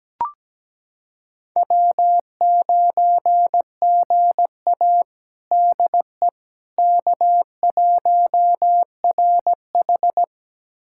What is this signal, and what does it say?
Morse code, heard as a keyed tone, about 17 wpm, 700 Hz. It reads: W9GA DE K1RH
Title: USA Callsign DE Callsign - 4 Characters - 17wpm